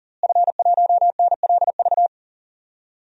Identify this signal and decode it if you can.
Morse code, heard as a keyed tone, about 40 wpm, 700 Hz. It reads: F1NLV
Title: DXCC-100 Call Signs - 5 Characters - Repeated 3x - 40wpm